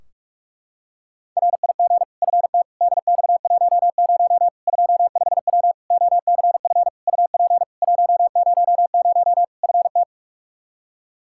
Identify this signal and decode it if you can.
Morse code, heard as a keyed tone, about 46 wpm, 700 Hz. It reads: RIG FT DX10 25W OCF UP 100 FT
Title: Practice QSO - Exchange 2 - Info - 45wpm